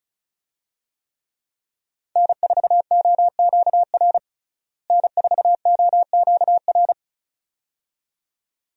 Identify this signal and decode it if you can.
Morse code, heard as a keyed tone, about 36 wpm, 700 Hz. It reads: N4OQR N4OQR
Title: USA Call Signs - 5 Characters - Repeated 2x - 35wpm